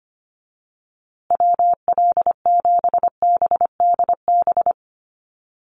Morse code, heard as a keyed tone, about 25 wpm, 700 Hz. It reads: WL7BDB
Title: USA Call Signs Repeated 2x 25wpm